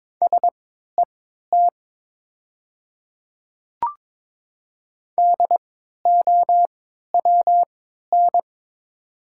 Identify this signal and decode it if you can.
Morse code, heard as a keyed tone, about 22 wpm, spaced as for 13 wpm, 700 Hz. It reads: SET DOWN
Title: 3x Character Spacing - Top 200 Words - 22wpm